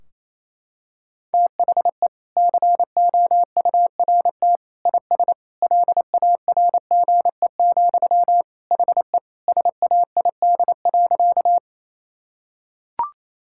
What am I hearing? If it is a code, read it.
Morse code, heard as a keyed tone, about 28 wpm, 700 Hz. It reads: THE COURT IS LARGE, HE SAID.
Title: Sentences from Top 1600 Words 28wpm